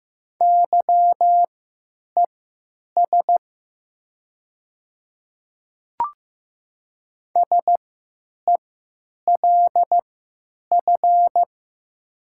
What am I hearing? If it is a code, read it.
Morse code, heard as a keyed tone, about 15 wpm, spaced as for 9 wpm, 700 Hz. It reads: YES SELF
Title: 3x Character Spacing - Top 500 Words - 15wpm